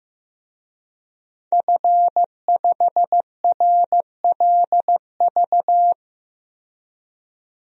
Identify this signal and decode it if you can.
Morse code, heard as a keyed tone, about 15 wpm, 700 Hz. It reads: F5RLV
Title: DXCC-100 Call Signs - 5 Characters - 15wpm